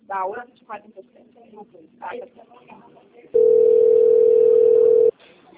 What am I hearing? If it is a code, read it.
Ringback tone (North America), 440 + 480 Hz, with speech.
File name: out- - - - - .97982